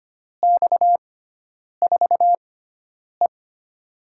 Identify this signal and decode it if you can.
Morse code, heard as a keyed tone, about 25 wpm, spaced as for 9 wpm, 700 Hz. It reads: X4E
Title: USA Call Signs - 4 Characters - 6x Character Spacing - 25wpm